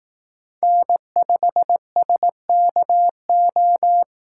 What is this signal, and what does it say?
Morse code, heard as a keyed tone, about 18 wpm, 700 Hz. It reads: N5SKO
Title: Speed Racing - USA 1x3 Call Signs 15wpm